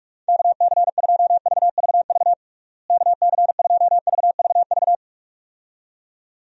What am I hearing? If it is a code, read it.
Morse code, heard as a keyed tone, about 46 wpm, 700 Hz. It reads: KC2VVV KC2VVV
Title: USA Call Signs - 6 Characters - Repeated 2x - 45wpm